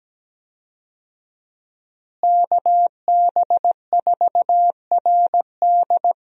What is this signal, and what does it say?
Morse code, heard as a keyed tone, about 17 wpm, 700 Hz. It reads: KB4RD
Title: USA Call Signs - 5 Characters - Repeated 2x - 17wpm